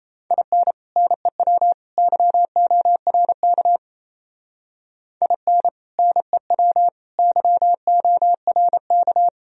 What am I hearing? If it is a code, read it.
Morse code, heard as a keyed tone, about 35 wpm, 700 Hz. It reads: IN NEW YORK IN NEW YORK
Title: Speed-Racing - Sets of 3 Words 28wpm